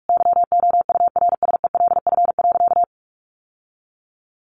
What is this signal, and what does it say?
Morse code, heard as a keyed tone, about 44 wpm, 700 Hz. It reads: YOURSELF.
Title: Sentences from Top 600 Words 45wpm